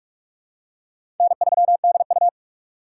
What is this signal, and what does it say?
Morse code, heard as a keyed tone, about 46 wpm, 700 Hz. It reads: N3DU
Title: USA Call Signs - 4 Characters - 4x Character Spacing - 45wpm